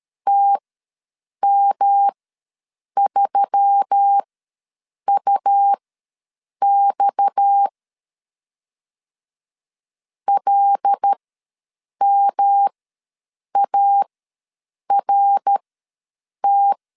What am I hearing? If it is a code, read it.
Morse code, heard as a keyed tone, about 13 wpm, spaced as for 7 wpm, 800 Hz. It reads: TM3UX LMART
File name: LEZIONI / 6 PKX0 / 6 PKX ws 12wpm 3106